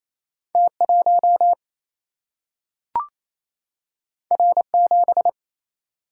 Morse code, heard as a keyed tone, about 28 wpm, 700 Hz. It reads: T1 R7
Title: Single-Letter Single-Number 28wpm